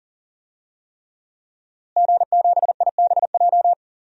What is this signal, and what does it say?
Morse code, heard as a keyed tone, about 40 wpm, 700 Hz. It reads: G7IBJ
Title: DXCC-100 Call Signs - 5 Characters - Repeated 3x - 40wpm